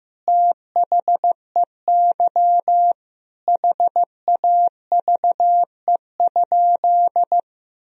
Morse code, heard as a keyed tone, about 15 wpm, 700 Hz. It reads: THEY HAVE?
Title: Sentences from Top 1000 Words - Review Part 3 - 15wpm